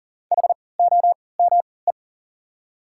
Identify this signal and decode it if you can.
Morse code, heard as a keyed tone, about 40 wpm, spaced as for 23 wpm, 700 Hz. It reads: HOME
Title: 3x Character Spacing - Top 300 Words - 40wpm